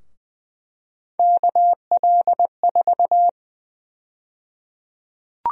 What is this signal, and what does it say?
Morse code, heard as a keyed tone, about 20 wpm, 700 Hz. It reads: KL4
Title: USA Call Sign Prefixes - Repeated 2x - 20wpm